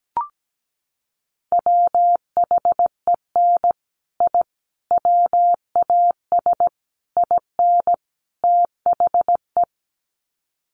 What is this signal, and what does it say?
Morse code, heard as a keyed tone, about 17 wpm, 700 Hz. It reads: WHEN I WAS IN THE
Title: Sets of 5 Words Encore2 17wpm